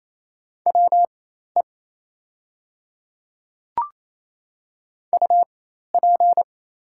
Morse code, heard as a keyed tone, about 27 wpm, spaced as for 13 wpm, 690 Hz. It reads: WE UP
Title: 4x Character Spacing - Top 100 Words - 28wpm